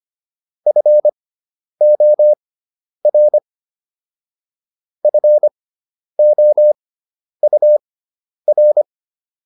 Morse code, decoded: FOR FOUR

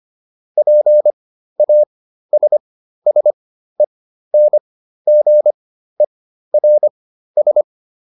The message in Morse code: PASSENGERS

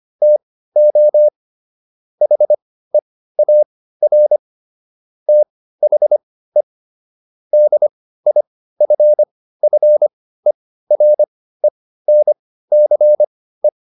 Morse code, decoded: TO HEAR THE DIFFERENCE

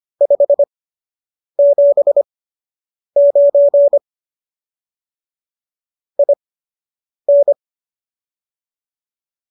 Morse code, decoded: 579 IN

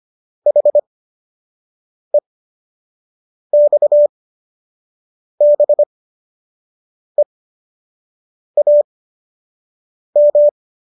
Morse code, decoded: HEXBEAM

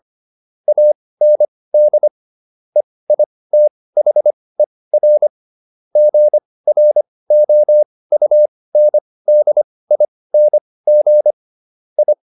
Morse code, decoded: AND EITHER GROUNDING I

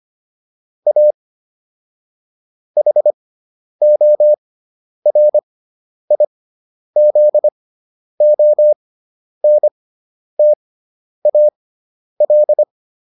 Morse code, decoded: A HORIZONTAL

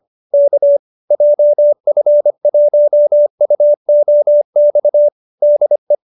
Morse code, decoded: K JF1UOX DE

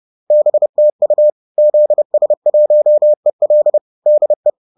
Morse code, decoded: BTU ZS1EL DE

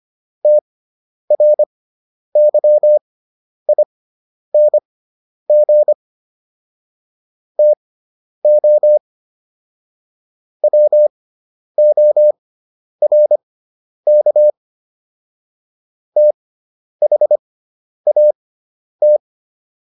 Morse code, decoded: TRYING TO WORK THAT

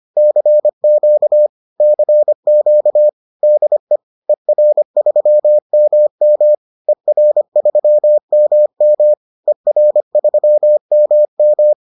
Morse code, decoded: CQ CQ DE ER3MM ER3MM ER3MM

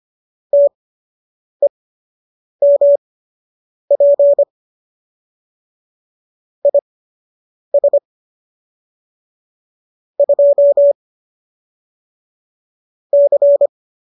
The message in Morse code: TEMP IS 2 C

